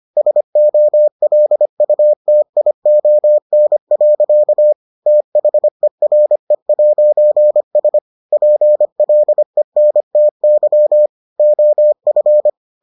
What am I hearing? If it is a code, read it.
Morse code, heard as a keyed tone, about 25 wpm, 600 Hz. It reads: SOLUTION. THERE'S PLENTY OF